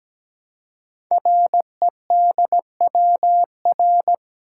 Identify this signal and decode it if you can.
Morse code, heard as a keyed tone, about 17 wpm, 700 Hz. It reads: REDWR
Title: Code Groups - 5 Letters - 17wpm